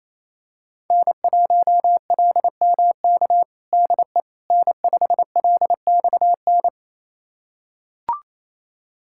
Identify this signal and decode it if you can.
Morse code, heard as a keyed tone, about 28 wpm, 700 Hz. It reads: N1LMK DE N5LXN
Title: USA Callsign DE Callsign - 5 Characters - 28wpm